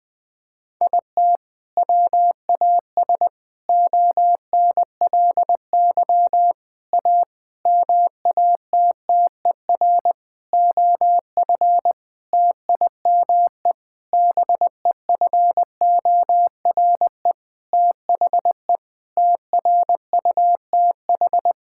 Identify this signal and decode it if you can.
Morse code, heard as a keyed tone, about 20 wpm, 700 Hz. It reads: IT WAS ONLY A MATTER OF TIME BEFORE THE TRUTH
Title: Sentences with 9-Word Phrases - 20wpm